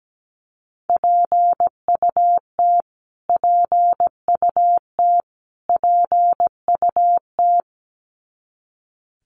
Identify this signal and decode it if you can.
Morse code, heard as a keyed tone, about 17 wpm, 700 Hz. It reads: PUT PUT PUT
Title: Top 300 Words Repeated 17wpm